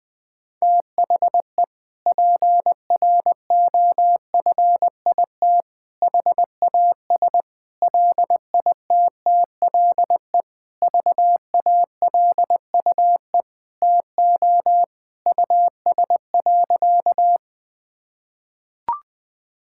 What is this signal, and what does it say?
Morse code, heard as a keyed tone, about 20 wpm, 700 Hz. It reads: THE PROFIT HAS LITTLE VALUE TO US.
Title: Sentences from Top 1000 Words - Review Part 5 - 20wpm